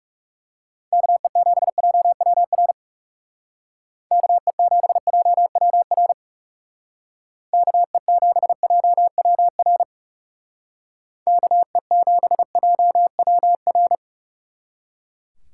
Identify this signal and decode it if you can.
Morse code, heard as a keyed tone, about 37 wpm, 700 Hz. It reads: KE7JWR KE7JWR KE7JWR KE7JWR
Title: Speed Racing - USA Call Signs 30wpm